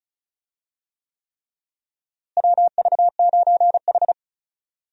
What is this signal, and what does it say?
Morse code, heard as a keyed tone, about 36 wpm, 700 Hz. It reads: WV9H
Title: USA Call Signs - 4 Characters - Repeated 3x - 35wpm